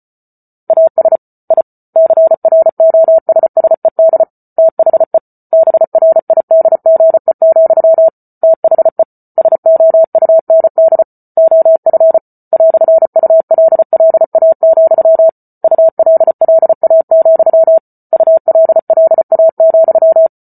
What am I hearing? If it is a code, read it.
Morse code, heard as a keyed tone, about 34 wpm, 650 Hz. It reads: AS I CROSSED THE BRIDGE, THE SOUND OF "ULLA, ULLA, ULLA,